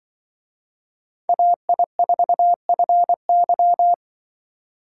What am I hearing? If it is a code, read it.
Morse code, heard as a keyed tone, about 24 wpm, 700 Hz. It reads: AI4FY
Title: Speed Racing - USA 2x2 Call Signs 20wpm